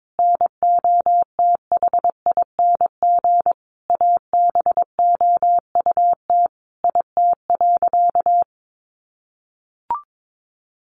Morse code, heard as a keyed tone, about 22 wpm, 700 Hz. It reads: NOTHING ABOUT IT.
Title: Sentences from Top 500 Words 22wpm